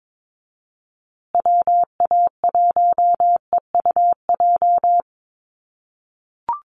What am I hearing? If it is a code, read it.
Morse code, heard as a keyed tone, about 22 wpm, 700 Hz. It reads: WA1EUJ
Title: USA Call Signs Repeated 3x 22wpm